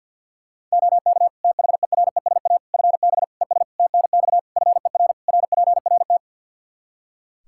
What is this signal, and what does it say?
Morse code, heard as a keyed tone, about 53 wpm, 700 Hz. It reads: OK THERESA FB ES TNX FER RPRT
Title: Practice QSO - Exchange 2 - Info - 50wpm